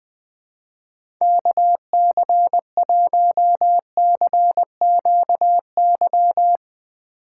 Morse code, heard as a keyed tone, about 20 wpm, 700 Hz. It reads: KC1CQY
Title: USA Call Signs - 6 Characters - 3x Character Spacing - 20wpm